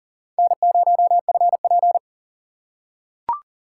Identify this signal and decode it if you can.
Morse code, heard as a keyed tone, about 40 wpm, 700 Hz. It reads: N0FP
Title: USA Call Signs - 4 Characters - 4x Character Spacing - 40wpm